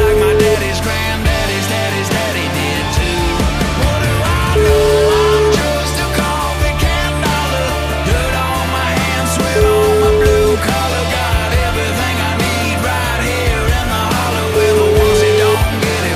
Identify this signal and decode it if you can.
Ringback tone (Europe, 425 Hz), with noise.